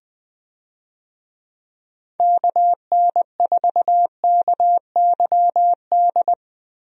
Morse code, heard as a keyed tone, about 20 wpm, 700 Hz. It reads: KN4KYD
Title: USA Call Signs - 6 Characters - Repeated 2x - 20wpm